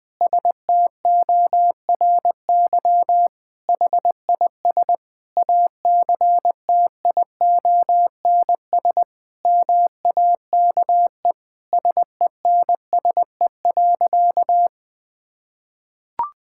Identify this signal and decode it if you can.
Morse code, heard as a keyed tone, about 21 wpm, 700 Hz. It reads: STORY HIS ACTIONS MAKE SENSE.
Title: Sentences with 5-Word Phrases - Spoken Beginning - 20wpm